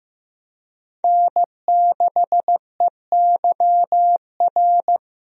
Morse code, heard as a keyed tone, about 15 wpm, 700 Hz. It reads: N6EYR
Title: USA Call Signs - 5 Characters - Repeated 2x - 15wpm